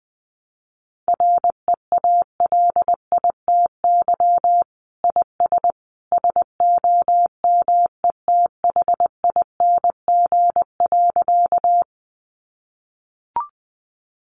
Morse code, heard as a keyed tone, about 20 wpm, 700 Hz. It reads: REALITY IS SOMETHING.
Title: Sentences from Top 1200 Words 20wpm